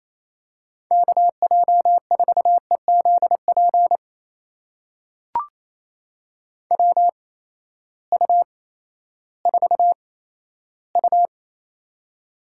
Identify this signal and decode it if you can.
Morse code, heard as a keyed tone, about 28 wpm, spaced as for 14 wpm, 700 Hz. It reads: KJ4EZP WU4U
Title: USA Call Signs - 8x Character Spacing - 28wpm